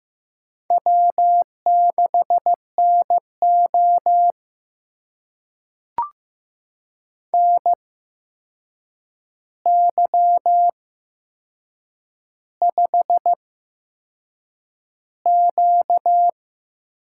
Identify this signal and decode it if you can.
Morse code, heard as a keyed tone, about 15 wpm, spaced as for 6 wpm, 700 Hz. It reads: W6NO NY5Q
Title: USA Call Signs - 4 Characters - 8x Character Spacing - 15wpm